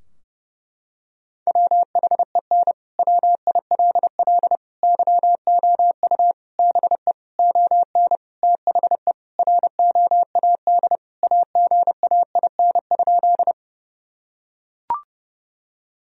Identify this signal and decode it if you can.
Morse code, heard as a keyed tone, about 30 wpm, 700 Hz. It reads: WHEN WILL YOU BE ON THE ROAD AGAIN?
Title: Sentences from Top 1000 Words - Review Part 2 - 30wpm